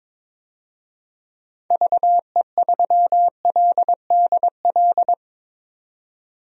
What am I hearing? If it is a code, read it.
Morse code, heard as a keyed tone, about 22 wpm, 700 Hz. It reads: VE3LDL
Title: DXCC-100 Call Signs - 6 Characters - 22wpm